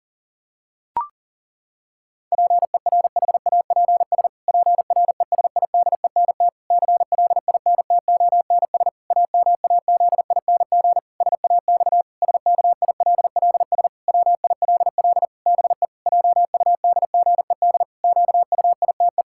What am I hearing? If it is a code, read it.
Morse code, heard as a keyed tone, about 37 wpm, 700 Hz. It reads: PERHAPS PRESIDENT CLINTONS AMAZING SAX SKILLS WILL BE JUDGED QUITE
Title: Pangram Challenge v2 - 40wpm